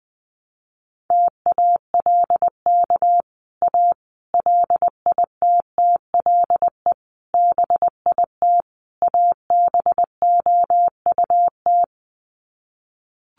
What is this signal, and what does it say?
Morse code, heard as a keyed tone, about 20 wpm, 700 Hz. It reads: TALK A LITTLE BIT ABOUT
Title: Sets of 5 Words Encore 20wpm